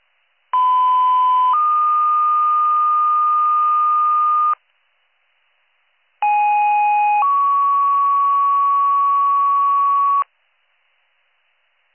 Two-tone page, heard as a one-second tone, then a three-second tone, about 1010 then 1220 Hz, about 830 then 1120 Hz. It is heard from 0.5 to 10.2 s.